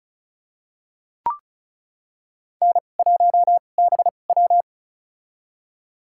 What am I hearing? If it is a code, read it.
Morse code, heard as a keyed tone, about 36 wpm, spaced as for 25 wpm, 700 Hz. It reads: N1BW
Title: USA Call Signs - 2x Character Spacing - 35wpm